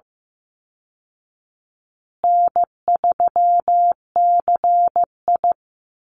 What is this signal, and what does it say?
Morse code, heard as a keyed tone, about 15 wpm, 700 Hz. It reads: N3CI
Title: Call Signs Repeated 2x - 15wpm